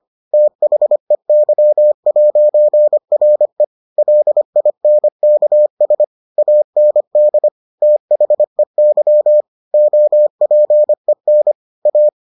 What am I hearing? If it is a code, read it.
Morse code, heard as a keyed tone, about 25 wpm, 600 Hz. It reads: THEY'RE LINKS AND THEY OPEN A